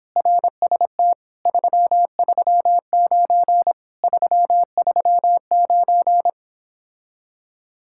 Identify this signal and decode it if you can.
Morse code, heard as a keyed tone, about 29 wpm, 700 Hz. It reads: RST 339 339